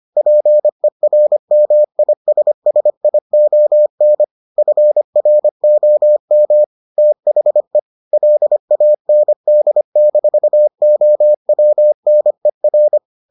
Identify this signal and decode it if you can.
Morse code, heard as a keyed tone, about 25 wpm, 600 Hz. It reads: PERMISSION FROM THE LAND-OWNER